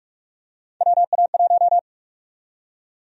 Morse code, heard as a keyed tone, about 46 wpm, 700 Hz. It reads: WA1
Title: USA Call Sign Prefixes - 45wpm